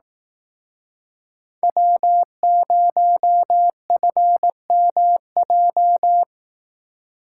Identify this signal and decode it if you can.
Morse code, heard as a keyed tone, about 18 wpm, 700 Hz. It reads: W0FMJ
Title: Speed Racing - USA 1x3 Call Signs 15wpm